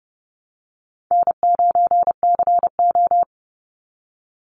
Morse code, heard as a keyed tone, about 30 wpm, 700 Hz. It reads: N9CO
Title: Easy Call Signs - 30wpm